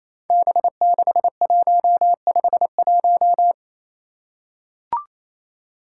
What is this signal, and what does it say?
Morse code, heard as a keyed tone, about 28 wpm, 700 Hz. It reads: B6151
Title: Five Challenging Letter-Number Combinations 28wpm